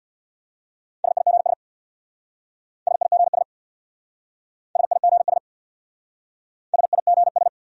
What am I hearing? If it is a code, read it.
Morse code, heard as a keyed tone, about 67 wpm, 700 Hz. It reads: SIGS SIGS SIGS SIGS